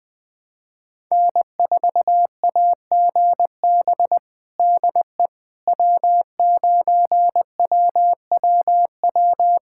Morse code, heard as a keyed tone, about 21 wpm, 700 Hz. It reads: N4AGB DE W9WWW
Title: USA Callsign DE Callsign - 5 Characters - 20wpm